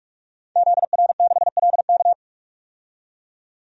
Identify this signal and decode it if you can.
Morse code, heard as a keyed tone, about 44 wpm, 690 Hz. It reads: ZR6LK
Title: DXCC-100 Call Signs - 5 Characters - 45wpm